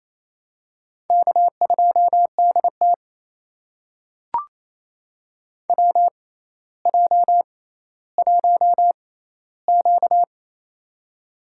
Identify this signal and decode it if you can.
Morse code, heard as a keyed tone, about 28 wpm, spaced as for 14 wpm, 700 Hz. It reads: K2DT WJ1Q
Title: USA Call Signs - 4 Characters - 6x Character Spacing - 28wpm